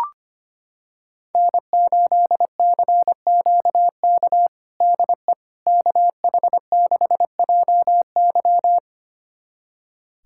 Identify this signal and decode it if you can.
Morse code, heard as a keyed tone, about 25 wpm, 700 Hz. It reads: N8CQK DE KH6JY
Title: USA Callsign DE Callsign - 5 Characters - 25wpm